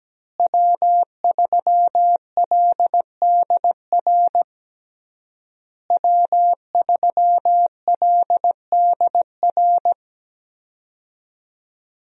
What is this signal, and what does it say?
Morse code, heard as a keyed tone, about 17 wpm, 700 Hz. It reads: W3LDR W3LDR
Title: USA Call Signs - 5 Characters - Repeated 3x - 17wpm